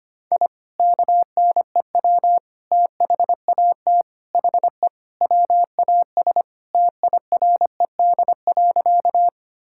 Morse code, heard as a keyed tone, about 25 wpm, 700 Hz. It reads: I KNEW THAT HE WAS TIRED.
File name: practice / 25wpm / Sentences with 3-Word Phrases - 25wpm